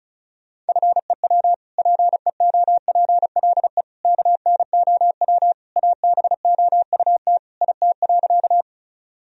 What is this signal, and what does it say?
Morse code, heard as a keyed tone, about 36 wpm, 700 Hz. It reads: FEW PEOPLE KNOW ABOUT IT.
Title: Sentences from Top 1000 Words - Review Part 1 - 35wpm